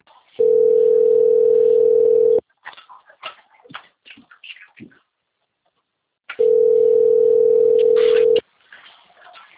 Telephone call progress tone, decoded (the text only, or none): ringback tone